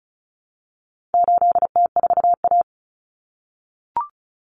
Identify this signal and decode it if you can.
Morse code, heard as a keyed tone, about 34 wpm, 700 Hz. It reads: 8T4A
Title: Common Cut Numbers - 4-Digit 35wpm